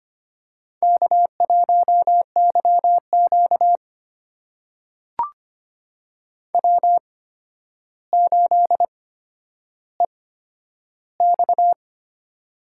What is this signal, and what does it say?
Morse code, heard as a keyed tone, about 25 wpm, spaced as for 11 wpm, 700 Hz. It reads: K1YQ W8EX
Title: USA Call Signs - 4 Characters - 8x Character Spacing - 25wpm